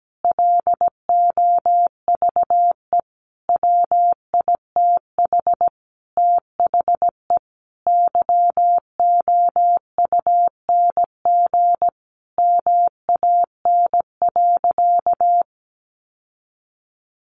Morse code, heard as a keyed tone, about 17 wpm, 700 Hz. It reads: LOVE WITH THE YOUNG MAN.
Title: Sentences from Top 900 Words 17wpm